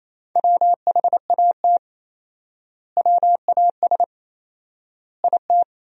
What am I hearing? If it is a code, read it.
Morse code, heard as a keyed tone, about 28 wpm, 700 Hz. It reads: WHAT WAS IT
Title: 4x Word-Spacing - Sentences from Top 300 Words - 28wpm